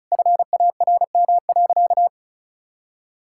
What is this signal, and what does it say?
Morse code, heard as a keyed tone, about 36 wpm, 700 Hz. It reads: FARM.
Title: Sentences from Top 1000 Words - Review Part 6 - 35wpm